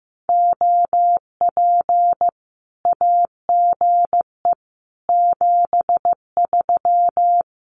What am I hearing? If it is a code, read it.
Morse code, heard as a keyed tone, about 15 wpm, 700 Hz. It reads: OP AGE 73
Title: Common QSO Phrases 15wpm